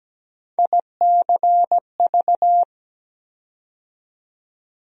Morse code, heard as a keyed tone, about 17 wpm, 700 Hz. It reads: ICV
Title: USA Call Sign Postfixes - 3 Letters 17wpm